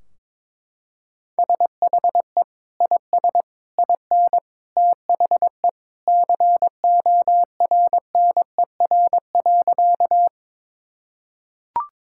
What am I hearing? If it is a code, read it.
Morse code, heard as a keyed tone, about 22 wpm, 700 Hz. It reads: SHE IS IN THE CORNER.
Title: Sentences from Top 1600 Words 22wpm